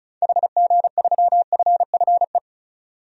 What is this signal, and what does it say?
Morse code, heard as a keyed tone, about 34 wpm, 700 Hz. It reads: HG3FFE
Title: DXCC-100 Call Signs - 6 Characters - 35wpm